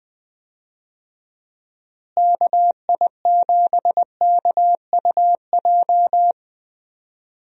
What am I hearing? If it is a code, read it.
Morse code, heard as a keyed tone, about 21 wpm, 700 Hz. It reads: KI7KUJ
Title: USA Call Signs - 6 Characters - Repeated 3x - 20wpm